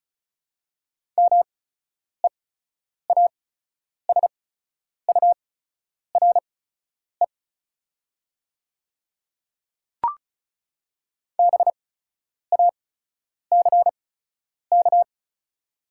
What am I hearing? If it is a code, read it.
Morse code, heard as a keyed tone, about 36 wpm, spaced as for 10 wpm, 690 Hz. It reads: MEASURE BACK
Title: 8x Character Spacing - Top 500 Words - 35wpm